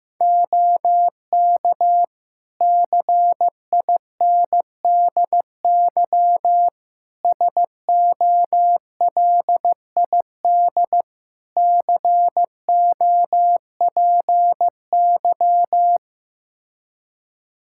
Morse code, heard as a keyed tone, about 15 wpm, 700 Hz. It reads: OK CINDY SOLID COPY
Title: K1USN SST QSO Practice 15wpm